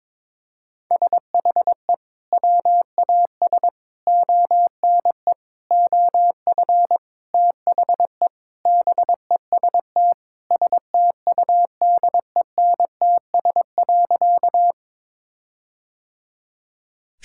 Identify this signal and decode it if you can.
Morse code, heard as a keyed tone, about 22 wpm, 700 Hz. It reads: SHE WAS ONE OF THE BEST STUDENTS.
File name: practice / 22wpm / Sentences with 4-Word Phrases - 22wpm